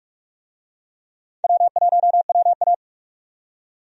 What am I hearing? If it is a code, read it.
Morse code, heard as a keyed tone, about 46 wpm, 700 Hz. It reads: W1WA
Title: USA Call Signs - 4 Characters - 45wpm